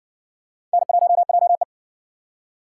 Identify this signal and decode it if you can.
Morse code, heard as a keyed tone, about 56 wpm, 700 Hz. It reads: N1JE